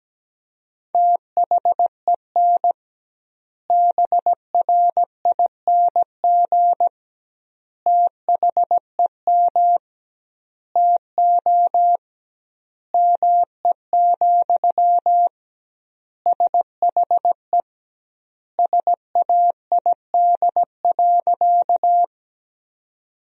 Morse code, heard as a keyed tone, about 17 wpm, 700 Hz. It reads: THEN BRING THEM TO ME, SHE SAID.
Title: 2x Word-Spacing - Sentences from Top 500 Words - 17wpm